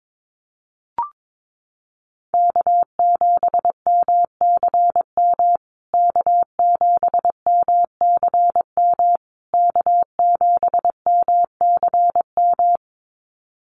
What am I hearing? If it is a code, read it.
Morse code, heard as a keyed tone, about 22 wpm, 700 Hz. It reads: K7MCM K7MCM K7MCM
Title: USA Call Signs Repeated 3x 22wpm